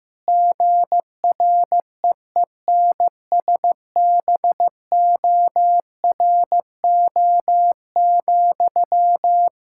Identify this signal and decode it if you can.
Morse code, heard as a keyed tone, about 16 wpm, 700 Hz. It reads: GREENSBORO,
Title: Practice QSO - Exchange 1 - Beginning - 15wpm